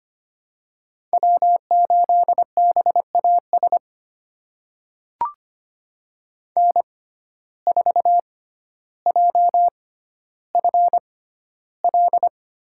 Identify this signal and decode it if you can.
Morse code, heard as a keyed tone, about 25 wpm, spaced as for 13 wpm, 700 Hz. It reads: W8BAS N4JFL
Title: USA Call Signs - 5 Characters - 6x Character Spacing - 25wpm